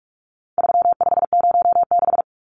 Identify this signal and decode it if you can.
Morse code, heard as a keyed tone, about 44 wpm, 700 Hz. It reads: 3506
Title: Numbers - 4-Digit 45wpm